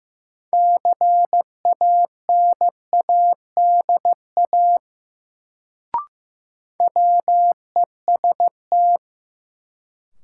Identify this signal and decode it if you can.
Morse code, heard as a keyed tone, about 15 wpm, 700 Hz. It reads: CANADA WEST
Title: Sets of 5 Words - New Word Drill - 15wpm